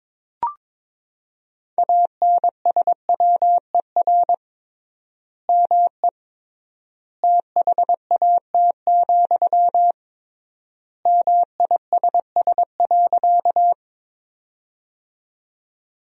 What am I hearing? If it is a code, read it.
Morse code, heard as a keyed tone, about 22 wpm, 700 Hz. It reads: ANSWER ME THAT, MISS.
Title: 3x Word-Spacing - Sentences from Top 500 Words - 22wpm